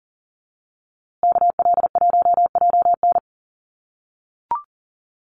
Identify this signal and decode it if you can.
Morse code, heard as a keyed tone, about 40 wpm, 700 Hz. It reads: KL1JN
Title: USA Call Signs Repeated 3x 40wpm